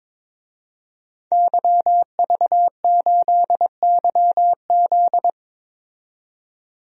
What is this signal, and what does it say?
Morse code, heard as a keyed tone, about 21 wpm, 700 Hz. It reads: YV8YZ
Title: DXCC-100 Call Signs - 5 Characters - 22wpm